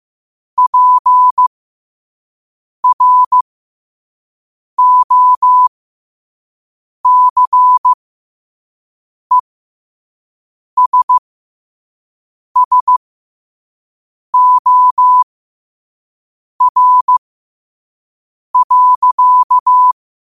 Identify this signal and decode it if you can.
Morse code, heard as a keyed tone, about 15 wpm, spaced as for 5 wpm, 1000 Hz. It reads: PROCESSOR.
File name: qotdfiles / 5wpm